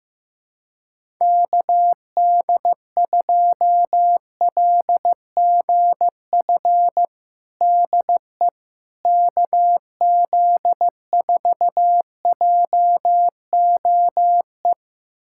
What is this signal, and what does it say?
Morse code, heard as a keyed tone, about 15 wpm, 700 Hz. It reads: KD2LGF DE KZ4JOE
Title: USA Callsign DE Callsign - 6 Characters - 15wpm